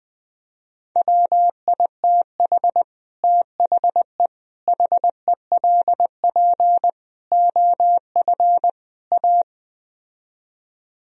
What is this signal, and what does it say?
Morse code, heard as a keyed tone, about 20 wpm, 700 Hz. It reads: WITH THE HELP OF A
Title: Speed-Racing - Sets of 5 Words 17wpm